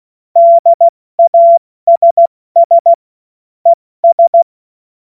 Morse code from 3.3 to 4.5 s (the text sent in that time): ES